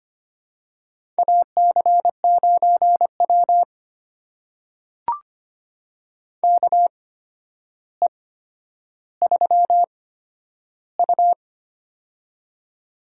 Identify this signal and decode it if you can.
Morse code, heard as a keyed tone, about 25 wpm, spaced as for 11 wpm, 700 Hz. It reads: AC9W KE3U
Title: USA Call Signs - 4 Characters - 8x Character Spacing - 25wpm